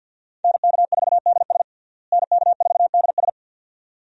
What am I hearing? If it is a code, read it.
Morse code, heard as a keyed tone, about 47 wpm, 700 Hz. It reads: NK4DS NK4DS
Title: USA Call Signs - 5 Characters - Repeated 2x - 50wpm